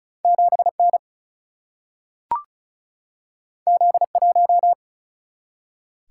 Morse code, heard as a keyed tone, about 36 wpm, 700 Hz. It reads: 7N Z1
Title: Single-Letter Single-Number 35wpm